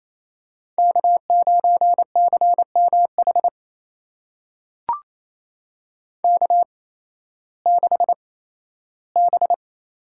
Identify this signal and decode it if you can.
Morse code, heard as a keyed tone, about 28 wpm, spaced as for 15 wpm, 700 Hz. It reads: K9CMH K6B
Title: USA Call Signs - 8x Character Spacing - 28wpm